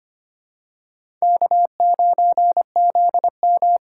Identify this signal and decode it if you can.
Morse code, heard as a keyed tone, about 25 wpm, 700 Hz. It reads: K9ZM
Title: USA Call Signs - 4 Characters - 6x Character Spacing - 25wpm